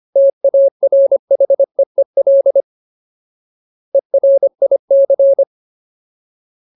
Morse code, heard as a keyed tone, about 25 wpm, 550 Hz. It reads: TARHEEL ERIC
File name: Ultimate Word List / 25WPM Ultimate